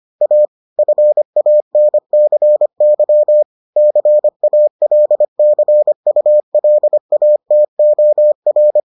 Morse code, heard as a keyed tone, about 25 wpm, 600 Hz. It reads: A FANCY CALCULATOR